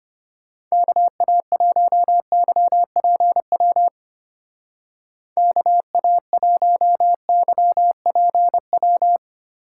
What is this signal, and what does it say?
Morse code, heard as a keyed tone, about 32 wpm, 700 Hz. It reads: KA1YPW KA1YPW